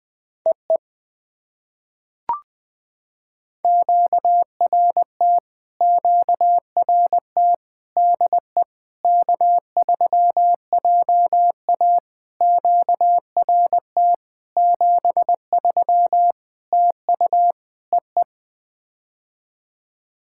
Morse code, heard as a keyed tone, about 20 wpm, 700 Hz. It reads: EE QRT QRT DE K3JA QRT 73 TU EE
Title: POTA QSO Practice - 20wpm